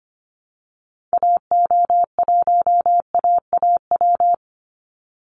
Morse code, heard as a keyed tone, about 25 wpm, 700 Hz. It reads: AO1AAW
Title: Call Signs Repeated 3x - 25wpm